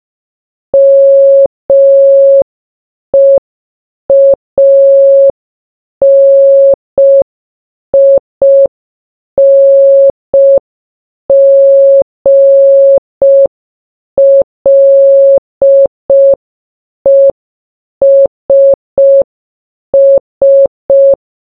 Morse code, heard as a keyed tone, about 5 wpm, 550 Hz. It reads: MEANINGLESS
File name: baba-is-you-the-folder-megapack-linux-macos-windows-gamerip-2024 / dobyxyjbym / 30. Morse